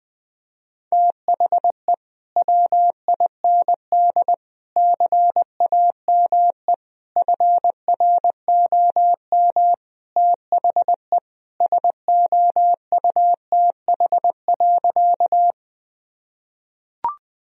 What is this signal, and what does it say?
Morse code, heard as a keyed tone, about 20 wpm, 700 Hz. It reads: THE WIND CAME FROM THE SOUTH.
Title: Sentences from Top 1000 Words - Review Part 2 - 20wpm